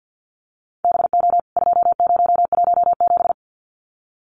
Morse code, heard as a keyed tone, about 50 wpm, 700 Hz. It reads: BO 2017
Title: CWT Exchange 50wpm